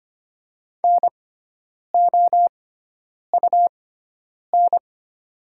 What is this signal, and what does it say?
Morse code, heard as a keyed tone, about 25 wpm, spaced as for 9 wpm, 700 Hz. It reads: NOUN
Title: 6x Character Spacing - Top 500 Words - 25wpm